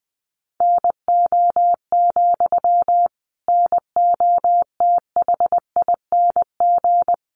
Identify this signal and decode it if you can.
Morse code, heard as a keyed tone, about 20 wpm, 700 Hz. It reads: NO, NOTHING
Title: Sentences from Top 800 Words 20wpm